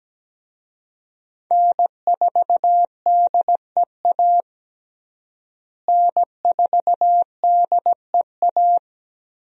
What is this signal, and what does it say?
Morse code, heard as a keyed tone, about 17 wpm, 700 Hz. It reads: N4DEA N4DEA
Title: USA Call Signs - 5 Characters - Repeated 3x - 17wpm